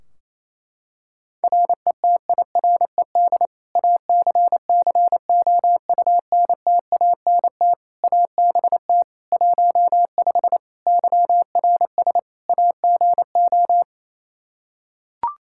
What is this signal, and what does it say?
Morse code, heard as a keyed tone, about 27 wpm, 700 Hz. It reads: RETIRED ACCOUNTANT ABT 15 YRS AGO
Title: Practice QSO - Exchange 3 - More Info - 28wpm